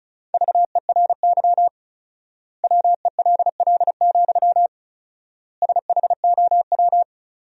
Morse code, heard as a keyed tone, about 34 wpm, 700 Hz. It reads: VERY WELL, SHOW
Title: 4x Word-Spacing - Sentences from Top 200 Words - 35wpm